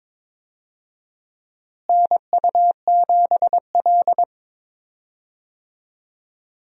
Morse code, heard as a keyed tone, about 22 wpm, 700 Hz. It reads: NU7L
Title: USA Call Signs - 4 Characters - Repeated 3x - 22wpm